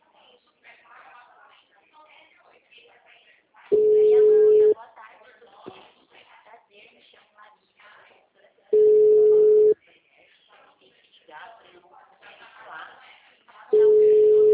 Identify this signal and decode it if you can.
Ringback tone (Europe, 425 Hz), with speech.
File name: out- -2003- - - .32286